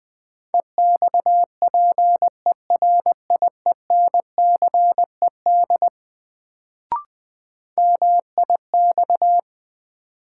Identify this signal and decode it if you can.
Morse code, heard as a keyed tone, about 21 wpm, 700 Hz. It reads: EXPERIENCED MIX
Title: Sentences from Top 1100 Words - New Word Drill Alt - 20wpm